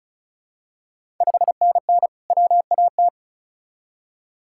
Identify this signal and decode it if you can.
Morse code, heard as a keyed tone, about 36 wpm, 700 Hz. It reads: 5NN WAT
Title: Ontario QSO Party Exchange 35wpm